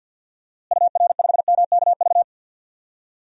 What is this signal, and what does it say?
Morse code, heard as a keyed tone, about 51 wpm, 700 Hz. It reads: UR5MKV
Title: DXCC-100 Call Signs - 6 Characters - Repeated 2x - 50wpm